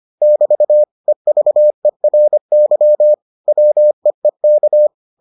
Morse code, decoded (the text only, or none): <BT> EVERY WEEK